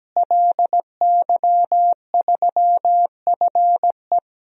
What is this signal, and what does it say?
Morse code, heard as a keyed tone, about 17 wpm, 700 Hz. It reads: LY3FE
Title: DXCC-100 Call Signs - 5 Characters - Repeated 3x - 17wpm